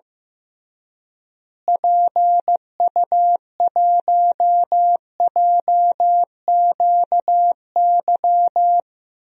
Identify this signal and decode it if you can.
Morse code, heard as a keyed tone, about 15 wpm, 700 Hz. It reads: PU1JQY
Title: DXCC-100 Call Signs - 6 Characters - Repeated 3x - 15wpm